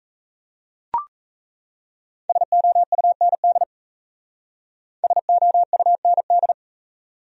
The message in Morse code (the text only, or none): SOUND SOUND